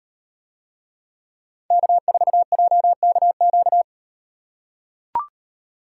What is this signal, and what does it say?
Morse code, heard as a keyed tone, about 38 wpm, 700 Hz. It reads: K4JKQ